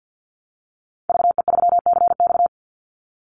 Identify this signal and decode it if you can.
Morse code, heard as a keyed tone, about 50 wpm, 700 Hz. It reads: VE3CX
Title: Call Signs Repeated 2x - 50wpm